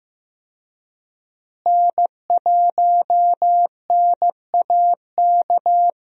Morse code, heard as a keyed tone, about 15 wpm, 700 Hz. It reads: N1NAK
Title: USA Call Signs - 5 Characters - Repeated 2x - 15wpm